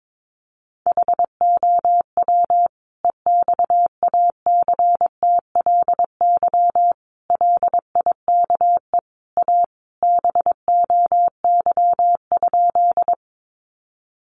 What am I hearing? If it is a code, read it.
Morse code, heard as a keyed tone, about 22 wpm, 700 Hz. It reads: HOW EXACTLY LIKE A BOY?
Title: Sentences from Top 900 Words 22wpm